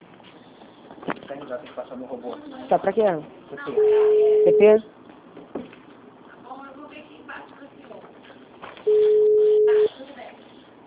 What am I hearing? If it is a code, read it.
Ringback tone (Europe, 425 Hz), with speech.